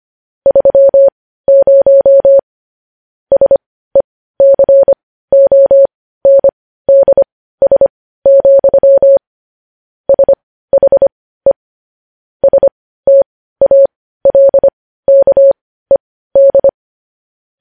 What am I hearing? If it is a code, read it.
Morse code, heard as a keyed tone, about 25 wpm, spaced as for 15 wpm, 550 Hz. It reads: 30 SECONDS, SHE STALKED
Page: News Headlines in Morse Code at 15 WPM